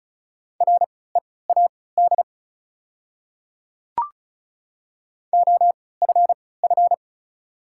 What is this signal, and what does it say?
Morse code, heard as a keyed tone, about 36 wpm, spaced as for 20 wpm, 700 Hz. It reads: READ OFF
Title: 3x Character Spacing - Top 300 Words - 35wpm